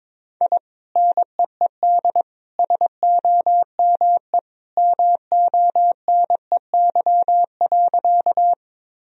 Morse code, heard as a keyed tone, about 21 wpm, 700 Hz. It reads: I NEED SOME MONEY.
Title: Sentences from Top 1000 Words - Review Part 1 - 22wpm